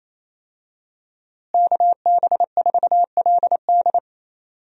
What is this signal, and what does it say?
Morse code, heard as a keyed tone, about 28 wpm, 700 Hz. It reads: KB4LD
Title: USA Call Signs - 5 Characters - Repeated 2x - 28wpm